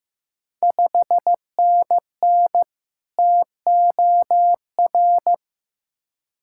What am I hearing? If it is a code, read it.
Morse code, heard as a keyed tone, about 16 wpm, 700 Hz. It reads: 5NN TOR
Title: Ontario QSO Party Exchange 15wpm